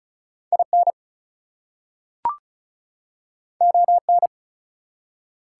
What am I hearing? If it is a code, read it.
Morse code, heard as a keyed tone, about 36 wpm, 700 Hz. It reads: IN ON
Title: Two Letter Words - 35wpm